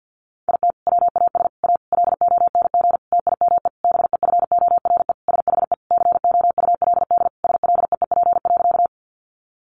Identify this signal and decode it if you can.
Morse code, heard as a keyed tone, about 50 wpm, 700 Hz. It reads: IT WAS A LONG TIME BEFORE SHE COULD SLEEP.
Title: Sentences from Top 500 Words 50wpm